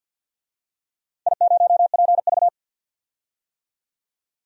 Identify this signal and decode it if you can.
Morse code, heard as a keyed tone, about 52 wpm, 700 Hz. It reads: I0PV
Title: DXCC-100 Call Signs - 4 Characters - 50wpm